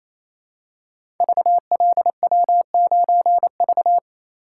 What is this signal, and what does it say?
Morse code, heard as a keyed tone, about 28 wpm, 700 Hz. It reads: VLW9V
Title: Code Groups - 5 Letter:Numbers - 28wpm